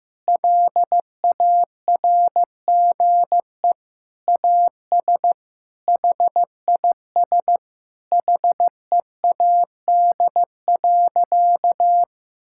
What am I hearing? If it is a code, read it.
Morse code, heard as a keyed tone, about 15 wpm, 700 Hz. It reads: LARGE AS HIS HEAD.
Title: Sentences from Top 1000 Words - Review Part 6 - 15wpm